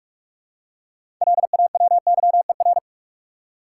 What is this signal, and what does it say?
Morse code, heard as a keyed tone, about 42 wpm, 700 Hz. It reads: LAWYER
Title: Six Letter Words - 45wpm